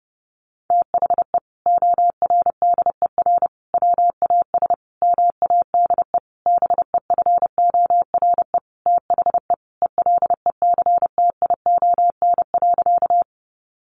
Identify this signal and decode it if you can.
Morse code, heard as a keyed tone, about 30 wpm, 700 Hz. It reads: THE ORDER WAS MADE BEFORE THE ELECTION.
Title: Sentences from Top 1500 Words 30wpm